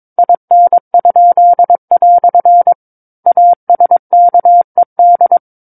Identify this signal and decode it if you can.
Morse code, heard as a keyed tone, about 22 wpm, 700 Hz. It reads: IN?" ASKED
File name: 06-Chapter-06 22 WPM